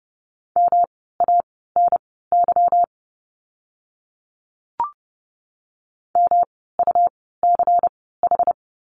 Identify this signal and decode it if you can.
Morse code, heard as a keyed tone, about 30 wpm, spaced as for 17 wpm, 700 Hz. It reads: MANY MUCH
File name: course / 30:10wpm / 230 Lesson 37f - Letter x - Words 30wpm